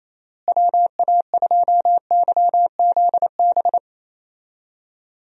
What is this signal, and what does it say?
Morse code, heard as a keyed tone, about 27 wpm, 700 Hz. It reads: WA2YZB
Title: USA Call Signs - 6 Characters - Repeated 3x - 28wpm